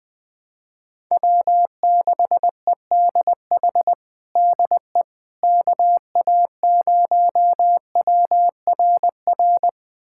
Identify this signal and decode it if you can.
Morse code, heard as a keyed tone, about 20 wpm, 700 Hz. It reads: W6EDH DE KA0WRR
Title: USA Callsign DE Callsign 20wpm